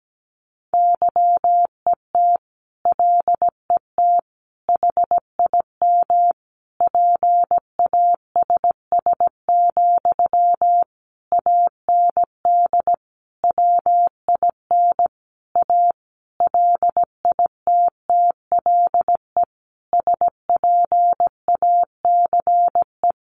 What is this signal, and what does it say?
Morse code, heard as a keyed tone, about 17 wpm, 700 Hz. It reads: YET LET HIM PASS, AND WIN A LITTLE SPACE
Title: Sentences from Top 1500 Words 17wpm